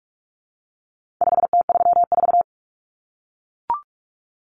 Morse code, heard as a keyed tone, about 44 wpm, 700 Hz. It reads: 5T34
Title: Common Cut Numbers - 4-Digit 45wpm